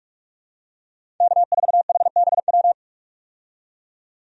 Morse code, heard as a keyed tone, about 46 wpm, 700 Hz. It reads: K4HBW
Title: USA Call Signs - 5 Characters - 45wpm